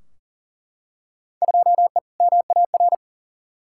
Morse code, heard as a keyed tone, about 37 wpm, 700 Hz. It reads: 2E MAR
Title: Field Day Exchanges Encore - 40wpm